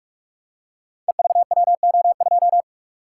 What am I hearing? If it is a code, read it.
Morse code, heard as a keyed tone, about 44 wpm, 700 Hz. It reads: EVWO2
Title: Code Groups - 5 Letter:Numbers - 45wpm